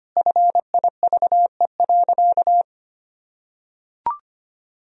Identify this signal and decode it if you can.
Morse code, heard as a keyed tone, about 25 wpm, 700 Hz. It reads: FIVE.
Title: Sentences from Top 400 Words and Abbreviations 25wpm